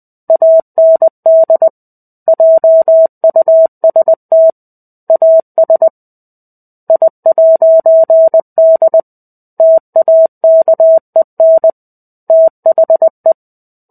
Morse code, heard as a keyed tone, about 20 wpm, 650 Hz. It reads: AND JUST AS I'D TAKEN THE
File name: 05-Chapter-05 20 WPM